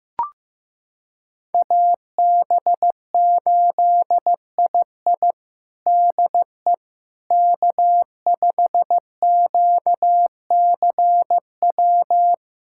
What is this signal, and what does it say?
Morse code, heard as a keyed tone, about 15 wpm, 700 Hz. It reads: AB8II DE K5QCW
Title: USA Callsign DE Callsign - 5 Characters - 15wpm